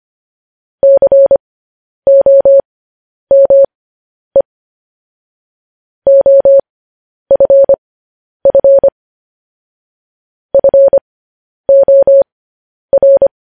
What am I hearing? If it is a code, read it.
Morse code, heard as a keyed tone, about 25 wpm, spaced as for 10 wpm, 550 Hz. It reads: COME OFF FOR
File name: podcast / News headlines-1x-25x10